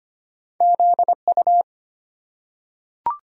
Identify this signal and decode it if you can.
Morse code, heard as a keyed tone, about 25 wpm, 700 Hz. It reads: ZU
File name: Two Letter-Number Combinations - 25wpm